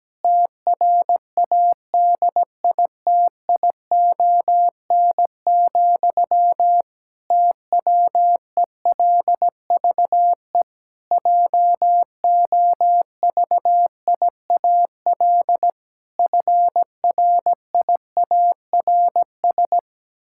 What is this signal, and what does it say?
Morse code, heard as a keyed tone, about 17 wpm, 700 Hz. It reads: TRADITION, TWELVE JOVIAL FRIARS
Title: Pangram Challenge v2 - 17wpm